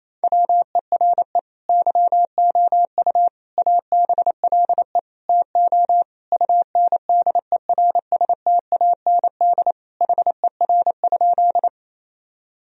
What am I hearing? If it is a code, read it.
Morse code, heard as a keyed tone, about 28 wpm, 700 Hz. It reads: WERE YOU ABLE TO UNDERSTAND HER?
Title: Sentences from Top 1000 Words - Review Part 2 - 28wpm